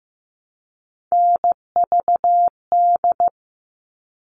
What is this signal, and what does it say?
Morse code, heard as a keyed tone, about 15 wpm, 700 Hz. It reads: NVD
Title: All Cut Numbers - 3-Digit 15wpm